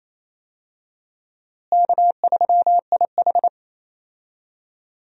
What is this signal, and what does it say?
Morse code, heard as a keyed tone, about 28 wpm, 700 Hz. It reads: K3IH
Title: USA Call Signs - 4 Characters - Repeated 3x - 28wpm